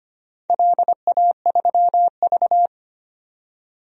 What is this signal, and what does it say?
Morse code, heard as a keyed tone, about 25 wpm, 700 Hz. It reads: LA3V
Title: DXCC-100 Call Signs - 4 Characters - 25wpm